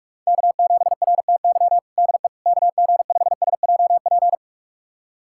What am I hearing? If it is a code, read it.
Morse code, heard as a keyed tone, about 41 wpm, 700 Hz. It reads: K7RTY DE KG5SJP
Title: USA Callsign DE Callsign 45wpm